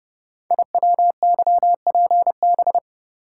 Morse code, heard as a keyed tone, about 30 wpm, 700 Hz. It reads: IWYPB
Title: Code Groups - 5 Letters - 30wpm